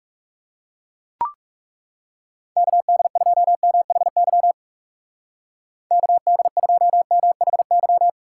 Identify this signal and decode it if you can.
Morse code, heard as a keyed tone, about 36 wpm, 700 Hz. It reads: KD2MHY KD2MHY
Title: Speed Racing - USA 2x3 Call Signs 30wpm